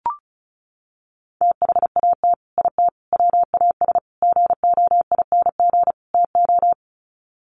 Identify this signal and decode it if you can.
Morse code, heard as a keyed tone, about 34 wpm, 700 Hz. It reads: THAT IT WAS GOING TO
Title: Sets of 5 Words Encore2 35wpm